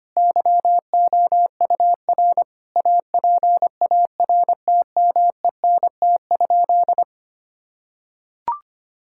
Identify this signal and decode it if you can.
Morse code, heard as a keyed tone, about 25 wpm, 700 Hz. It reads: YOUR APARTMENT?
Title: Sentences from Top 1000 Words - Review Part 6 - 25wpm